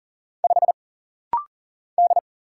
Morse code, heard as a keyed tone, about 40 wpm, 700 Hz. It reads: 5 D
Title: Single Letter-Number Alt - Rapid-Fire 40wpm